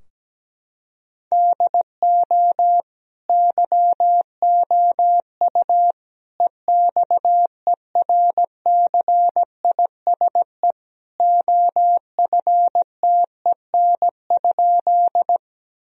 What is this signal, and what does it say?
Morse code, heard as a keyed tone, about 17 wpm, 700 Hz. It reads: DO YOU EXERCISE OFTEN?
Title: Sentences from Top 1000 Words - Review Part 4 - 17wpm